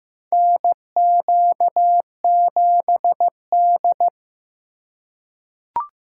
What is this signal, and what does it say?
Morse code, heard as a keyed tone, about 15 wpm, 700 Hz. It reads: NQ7D
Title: USA Call Signs - 4 Characters - 4x Character Spacing - 15wpm